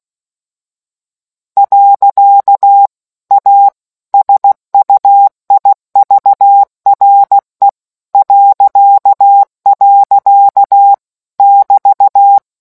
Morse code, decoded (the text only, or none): . A SUIVRE .. <BT>